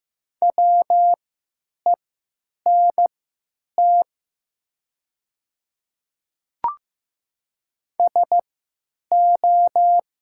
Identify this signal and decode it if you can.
Morse code, heard as a keyed tone, about 15 wpm, spaced as for 9 wpm, 700 Hz. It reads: WENT SO
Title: 3x Character Spacing - Top 300 Words - 15wpm